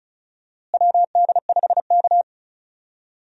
Morse code, heard as a keyed tone, about 36 wpm, 700 Hz. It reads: WD5K
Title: USA Call Signs - 4 Characters - Repeated 3x - 35wpm